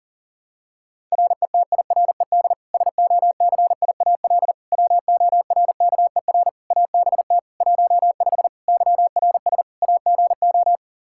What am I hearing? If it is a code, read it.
Morse code, heard as a keyed tone, about 40 wpm, 700 Hz. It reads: RETIRED SOCIAL WORKER ABT 15 YRS AGO
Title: Practice QSO - Exchange 3 - More Info - 40wpm